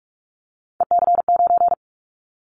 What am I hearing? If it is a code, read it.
Morse code, heard as a keyed tone, about 44 wpm, 700 Hz. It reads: EC9
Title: Callsign Prefixes for Top 100 DXCC - 45wpm